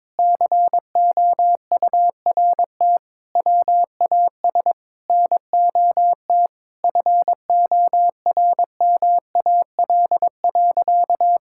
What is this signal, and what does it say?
Morse code, heard as a keyed tone, about 22 wpm, 700 Hz. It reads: COURT WAS NOT FORMAL.
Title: Sentences from Top 1600 Words 22wpm